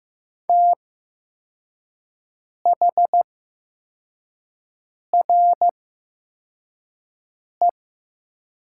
Morse code, heard as a keyed tone, about 15 wpm, spaced as for 4 wpm, 700 Hz. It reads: THRE
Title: 8x Character Spacing - Top 200 Words - 15wpm